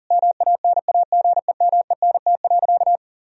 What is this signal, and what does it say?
Morse code, heard as a keyed tone, about 40 wpm, 700 Hz. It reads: MANAGEMENT.
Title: Sentences from Top 1000 Words - Review Part 4 - 40wpm